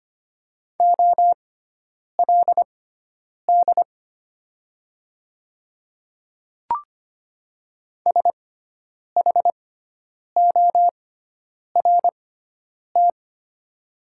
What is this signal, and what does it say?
Morse code, heard as a keyed tone, about 25 wpm, spaced as for 9 wpm, 700 Hz. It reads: OLD SHORT
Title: 6x Character Spacing - Top 500 Words - 25wpm